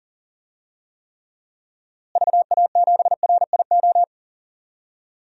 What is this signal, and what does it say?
Morse code, heard as a keyed tone, about 40 wpm, 700 Hz. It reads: VA7RIO
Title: DXCC-100 Call Signs - 6 Characters - Repeated 2x - 40wpm